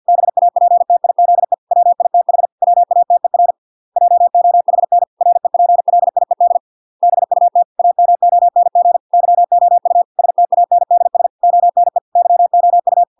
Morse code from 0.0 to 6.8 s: BAPTIZE WITH WATER JOHN REPLIED